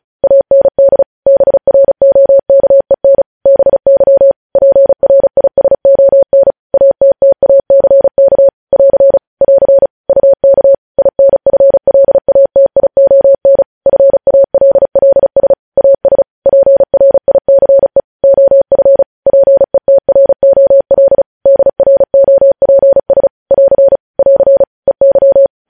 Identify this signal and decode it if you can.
Morse code, heard as a keyed tone, about 36 wpm, 550 Hz. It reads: AND BROKEN BY PRISON ATTACK <AR> <AR> UK INFLATION FALLS AS PRICE OF PETROL DROPS <AR> <AR> EY
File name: podcast / News headlines-1x-35x35